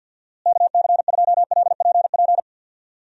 Morse code, heard as a keyed tone, about 50 wpm, 700 Hz. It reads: KC2LPP
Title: USA Call Signs - 6 Characters - Repeated 3x - 50wpm